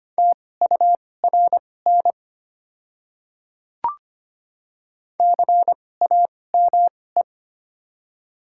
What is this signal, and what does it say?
Morse code, heard as a keyed tone, about 25 wpm, spaced as for 18 wpm, 700 Hz. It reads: TURN CAME
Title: 2x Character Spacing - Top 300 Words - 25wpm